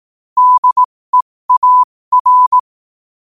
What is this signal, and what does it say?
Morse code, heard as a keyed tone, about 18 wpm, spaced as for 15 wpm, 1000 Hz. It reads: DEAR
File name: qotdfiles / 15wpm